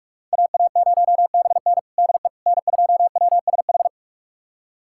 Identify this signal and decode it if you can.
Morse code, heard as a keyed tone, about 46 wpm, 700 Hz. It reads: AA0BN DE N2WSH
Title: USA Callsign DE Callsign - 5 Characters - 45wpm